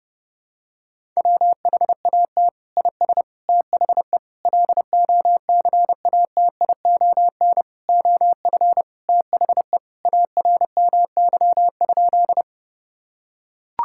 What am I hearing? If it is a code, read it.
Morse code, heard as a keyed tone, about 28 wpm, 700 Hz. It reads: WHAT IS THE LOCATION OF THE ARMY?
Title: Sentences from Top 1000 Words - Review Part 6 - 30wpm